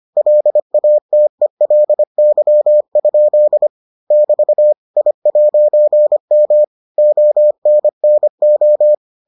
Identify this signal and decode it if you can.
Morse code, heard as a keyed tone, about 25 wpm, 600 Hz. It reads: LATELY? <BT> I'M ONNO